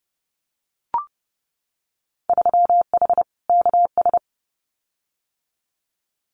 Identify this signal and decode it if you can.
Morse code, heard as a keyed tone, about 30 wpm, 700 Hz. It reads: 3H KS
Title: Winter Field Day Exchanges 30wpm